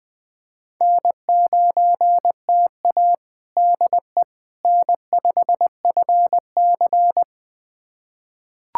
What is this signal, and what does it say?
Morse code, heard as a keyed tone, about 20 wpm, 700 Hz. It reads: N9TA DE N5FC
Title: USA Callsign DE Callsign - 4 Characters - 20wpm